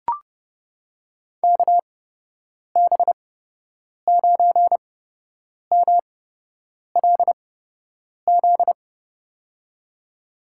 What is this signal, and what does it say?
Morse code, heard as a keyed tone, about 32 wpm, spaced as for 8 wpm, 700 Hz. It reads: KB9MLZ
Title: USA Call Signs - 6 Characters - 8x Character Spacing - 30wpm